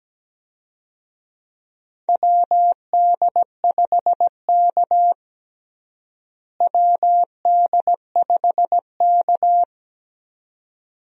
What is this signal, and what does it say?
Morse code, heard as a keyed tone, about 17 wpm, 700 Hz. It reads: WD5K WD5K
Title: USA Call Signs - 4 Characters - Repeated 2x - 17wpm